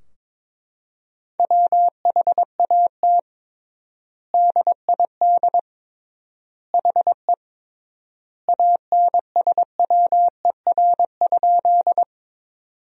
Morse code, heard as a keyed tone, about 22 wpm, 700 Hz. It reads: WHAT DID HE ANSWER?
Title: 3x Word-Spacing - Sentences from Top 300 Words - 22wpm